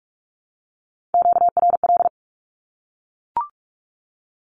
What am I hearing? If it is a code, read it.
Morse code, heard as a keyed tone, about 44 wpm, 700 Hz. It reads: QRL
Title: Q-Codes Repeated 3x 45wpm